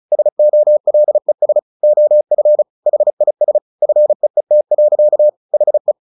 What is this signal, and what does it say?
Morse code, heard as a keyed tone, about 34 wpm, 600 Hz. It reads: SOLES OF HIS FEET. HE
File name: the old man and the sea / the old man and the sea 1 35wpm